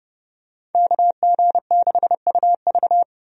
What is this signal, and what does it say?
Morse code, heard as a keyed tone, about 30 wpm, 700 Hz. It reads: KG6UV
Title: USA Call Signs - 5 Characters - 30wpm